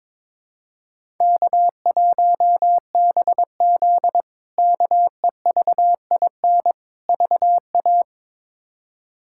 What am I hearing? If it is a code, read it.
Morse code, heard as a keyed tone, about 22 wpm, 700 Hz. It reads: K1BZ KEVIN VA
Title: K1USN SST QSO Practice - 22wpm